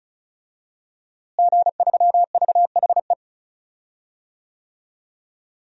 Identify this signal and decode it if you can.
Morse code, heard as a keyed tone, about 36 wpm, 700 Hz. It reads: G3VHE
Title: DXCC-100 Call Signs - 5 Characters - Repeated 2x - 35wpm